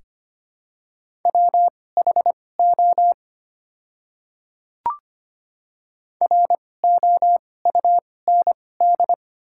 Morse code, heard as a keyed tone, about 25 wpm, spaced as for 18 wpm, 700 Hz. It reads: WHO ROUND
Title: 2x Character Spacing - Top 200 Words - 25wpm